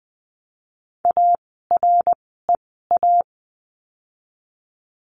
Morse code, heard as a keyed tone, about 20 wpm, spaced as for 14 wpm, 700 Hz. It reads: AREA
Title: 108 Lesson 18f - Letter m - Words 20wpm